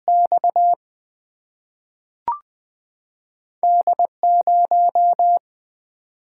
Morse code, heard as a keyed tone, about 21 wpm, 700 Hz. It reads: X D 0
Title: Single-Letter Single-Number 20wpm